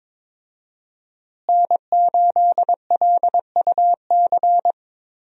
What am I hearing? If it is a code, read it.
Morse code, heard as a keyed tone, about 22 wpm, 700 Hz. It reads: N8LUC
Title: USA Call Signs - 5 Characters - Repeated 2x - 22wpm